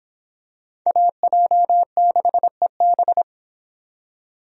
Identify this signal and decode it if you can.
Morse code, heard as a keyed tone, about 26 wpm, 700 Hz. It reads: AJ6EB